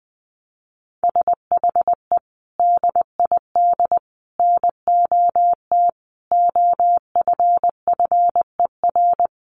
Morse code, heard as a keyed tone, about 20 wpm, 700 Hz. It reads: SHE DID NOT OFFER
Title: Sentences from Top 1500 Words 20wpm